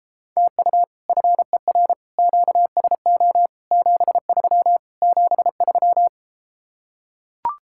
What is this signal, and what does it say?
Morse code, heard as a keyed tone, about 33 wpm, 700 Hz. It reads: TU FER QSO 73 73